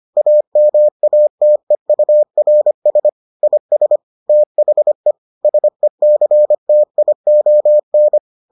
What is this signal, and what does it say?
Morse code, heard as a keyed tone, about 25 wpm, 600 Hz. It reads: AMATEURS IS THE SECTION